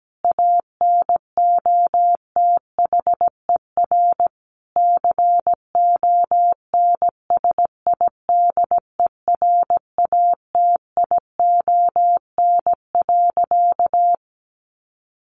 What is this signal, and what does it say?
Morse code, heard as a keyed tone, about 17 wpm, 700 Hz. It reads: ANOTHER CONSIDERATION.
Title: Sentences from Top 1500 Words 17wpm